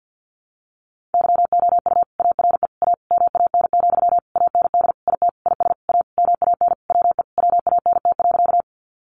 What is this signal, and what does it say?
Morse code, heard as a keyed tone, about 50 wpm, 700 Hz. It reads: YOU ARE A MAN, AND IT IS A MAN WE WANT.
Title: Sentences from Top 200 Words 50wpm